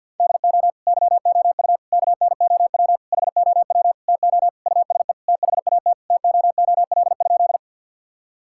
Morse code, heard as a keyed tone, about 50 wpm, 700 Hz. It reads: DO YOU KNOW HOW TO USE THAT TOOL?
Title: Sentences from Top 1000 Words - Review Part 6 - 50wpm